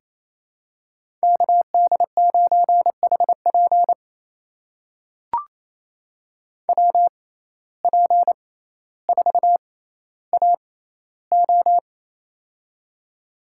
Morse code, heard as a keyed tone, about 28 wpm, spaced as for 14 wpm, 700 Hz. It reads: KD9HP WP4AO
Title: USA Call Signs - 5 Characters - 6x Character Spacing - 28wpm